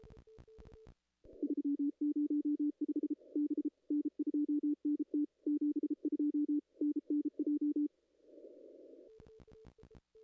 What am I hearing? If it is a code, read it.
Morse code, heard as a keyed tone, about 33 wpm, 300 Hz. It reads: 305 B N2NT 72 NNJ
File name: contests / ss-cw-k5tr-op-n5rz -2300